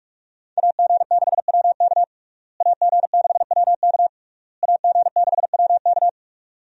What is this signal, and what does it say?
Morse code, heard as a keyed tone, about 41 wpm, 700 Hz. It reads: AG6WK AG6WK AG6WK
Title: USA Call Signs - 5 Characters - Repeated 3x - 45wpm